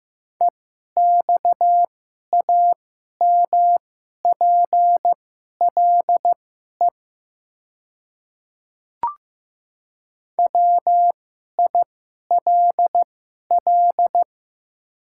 Morse code, heard as a keyed tone, about 15 wpm, spaced as for 11 wpm, 700 Hz. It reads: EXAMPLE WILL
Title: 2x Character Spacing - Top 300 Words - 15wpm